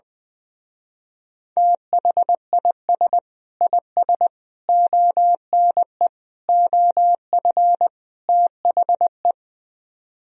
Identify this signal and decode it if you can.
Morse code, heard as a keyed tone, about 20 wpm, 700 Hz. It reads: THIS IS ONE OF THE
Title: Speed-Racing - Sets of 5 Words 20wpm